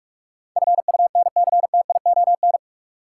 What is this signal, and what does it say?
Morse code, heard as a keyed tone, about 45 wpm, 700 Hz. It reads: FUNCTION
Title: Sentences from Top 1100 Words - New Word Drill - 45wpm